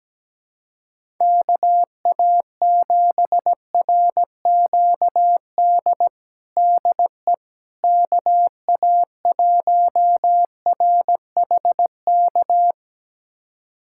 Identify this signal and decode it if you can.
Morse code, heard as a keyed tone, about 17 wpm, 700 Hz. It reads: KA7RQD DE KA1RHK
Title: USA Callsign DE Callsign - 6 Characters - 17wpm